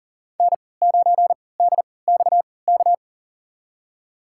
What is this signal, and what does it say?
Morse code, heard as a keyed tone, about 40 wpm, spaced as for 23 wpm, 700 Hz. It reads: N9DXK
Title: USA Call Signs - 5 Characters - 3x Character Spacing - 40wpm